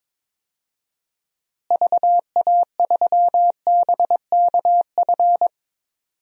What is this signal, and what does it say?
Morse code, heard as a keyed tone, about 22 wpm, 700 Hz. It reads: VA3BKF
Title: DXCC-100 Call Signs - 6 Characters - Repeated 3x - 22wpm